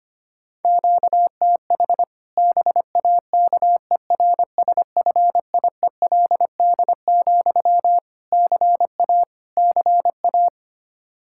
Morse code, heard as a keyed tone, about 25 wpm, 700 Hz. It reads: QTH BAKERSFIELD, CA CA
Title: Practice QSO - Exchange 1 - Beginning - 25wpm